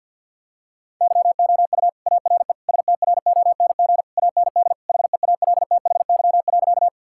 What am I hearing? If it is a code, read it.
Morse code, heard as a keyed tone, about 50 wpm, 700 Hz. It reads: YOU ARE STRONG AND HEALTHY.
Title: Sentences from Top 1000 Words - Review Part 5 - 50wpm